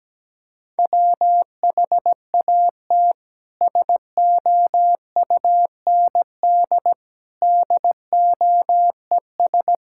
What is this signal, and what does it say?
Morse code, heard as a keyed tone, about 17 wpm, 700 Hz. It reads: WHAT SOUND DOES
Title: Sentences from Top 1000 Words - Review Part 5 - 17wpm